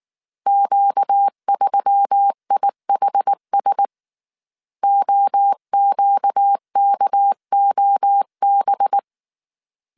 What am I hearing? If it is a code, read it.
Morse code, heard as a keyed tone, about 19 wpm, 800 Hz. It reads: Q3IHS OQKOB
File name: CW / DET / 1 6 1500C S A 18wpm 2255